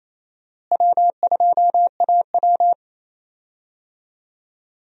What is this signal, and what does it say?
Morse code, heard as a keyed tone, about 28 wpm, 700 Hz. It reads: W2AW
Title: USA Call Signs - 4 Characters - Repeated 3x - 28wpm